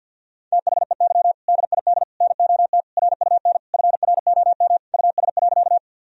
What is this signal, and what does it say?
Morse code, heard as a keyed tone, about 50 wpm, 700 Hz. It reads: THEY DID NOT RUN FROM US.
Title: Sentences from Top 1000 Words - Review Part 1 - 50wpm